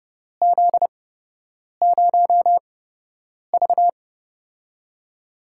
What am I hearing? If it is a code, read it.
Morse code, heard as a keyed tone, about 30 wpm, spaced as for 8 wpm, 700 Hz. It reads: Z0V
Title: USA Call Signs - 8x Character Spacing - 30wpm